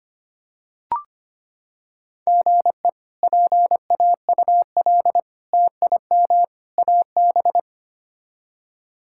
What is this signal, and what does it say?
Morse code, heard as a keyed tone, about 25 wpm, 700 Hz. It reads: GE PAUL TIM AB
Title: K1USN SST QSO Practice - 25wpm